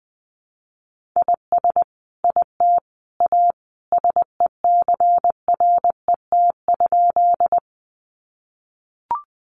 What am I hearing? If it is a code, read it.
Morse code, heard as a keyed tone, about 20 wpm, 700 Hz. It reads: IS IT A SECRET?
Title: Sentences from Top 1200 Words 20wpm